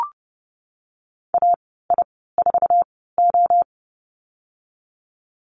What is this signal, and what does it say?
Morse code, heard as a keyed tone, about 30 wpm, spaced as for 17 wpm, 700 Hz. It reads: AI4O
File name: Lesson 14g - Number 2 - Call Signs 30wpm